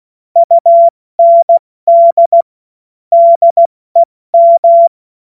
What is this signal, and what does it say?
Morse code, heard as a keyed tone, about 16 wpm, 680 Hz. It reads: UND DEM
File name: Morsekurs / Witze / Witz 14.November 2023 70 F80